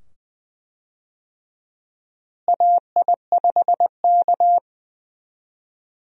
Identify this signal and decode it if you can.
Morse code, heard as a keyed tone, about 20 wpm, 700 Hz. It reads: AI5K
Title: USA Call Signs - 4 Characters - Repeated 2x - 20wpm